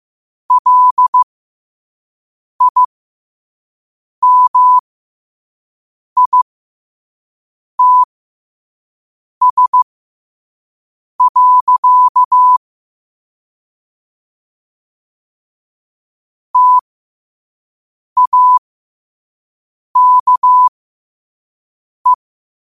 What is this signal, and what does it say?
Morse code, heard as a keyed tone, about 15 wpm, spaced as for 5 wpm, 1000 Hz. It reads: LIMITS. TAKE
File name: qotdfiles / 5wpm